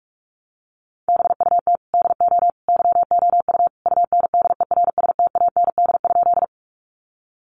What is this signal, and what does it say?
Morse code, heard as a keyed tone, about 44 wpm, 700 Hz. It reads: BUT DO YOU UNDERSTAND?
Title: Sentences from Top 600 Words 45wpm